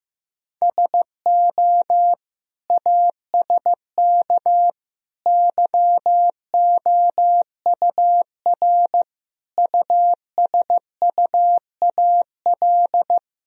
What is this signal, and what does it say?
Morse code, heard as a keyed tone, about 16 wpm, 700 Hz. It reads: SO ASK YOUR USUAL
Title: Sentences from Top 1000 Words - Review Part 2 - 15wpm